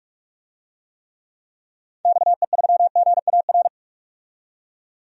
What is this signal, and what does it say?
Morse code, heard as a keyed tone, about 45 wpm, 700 Hz. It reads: XE3GAR
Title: DXCC-100 Call Signs - 6 Characters - 45wpm